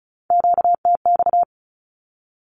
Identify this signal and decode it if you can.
Morse code, heard as a keyed tone, about 34 wpm, 700 Hz. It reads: QTX
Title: Q-Codes 35wpm